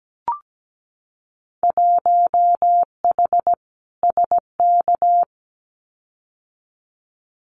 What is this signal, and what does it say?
Morse code, heard as a keyed tone, about 17 wpm, 700 Hz. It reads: 1H SK
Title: Winter Field Day Exchanges 17wpm